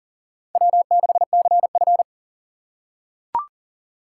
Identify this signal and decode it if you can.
Morse code, heard as a keyed tone, about 40 wpm, 700 Hz. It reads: W6CF
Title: USA Call Signs - 4 Characters - 6x Character Spacing - 40wpm